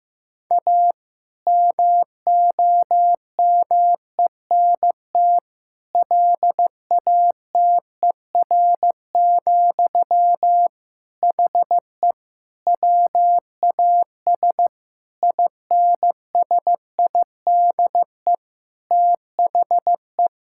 Morse code, heard as a keyed tone, about 16 wpm, 700 Hz. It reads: A MOMENT LATER, HE WAS INSIDE THE
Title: Sentences from Top 1000 Words - Review Part 6 - 15wpm